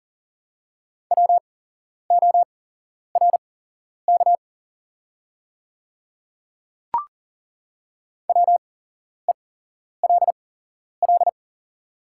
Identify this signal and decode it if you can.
Morse code, heard as a keyed tone, about 39 wpm, spaced as for 11 wpm, 690 Hz. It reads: WORK WELL
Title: 8x Character Spacing - Top 200 Words - 40wpm